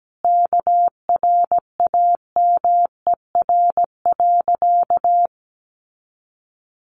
Morse code, heard as a keyed tone, about 17 wpm, 700 Hz. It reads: KRAMER.
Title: A Sound Of Thunder 17wpm